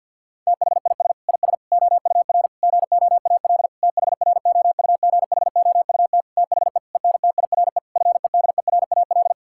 Morse code, heard as a keyed tone, about 50 wpm, 700 Hz. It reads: THIS IS OUR GOAL THROUGHOUT THE ENTIRE FEDERAL
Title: Sentences from Top 1000 Words - Review Part 5 - 50wpm